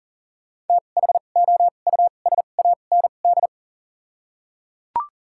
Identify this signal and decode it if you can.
Morse code, heard as a keyed tone, about 38 wpm, spaced as for 28 wpm, 700 Hz. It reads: THOUSAND
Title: 2x Character Spacing - Top 500 Words - 40wpm